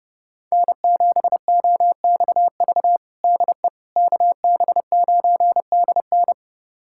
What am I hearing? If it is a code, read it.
Morse code, heard as a keyed tone, about 31 wpm, 700 Hz. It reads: N7OXV DE KB9DN
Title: USA Callsign DE Callsign - 5 Characters - 30wpm